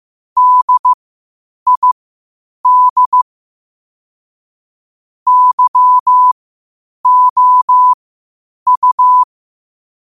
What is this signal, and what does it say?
Morse code, heard as a keyed tone, about 15 wpm, spaced as for 8 wpm, 1000 Hz. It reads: DID YOU
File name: qotdfiles / 8wpm